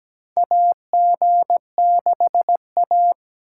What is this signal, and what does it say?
Morse code, heard as a keyed tone, about 17 wpm, 700 Hz. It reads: AG6A
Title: USA Call Signs - 4 Characters - 2x Character Spacing - 17wpm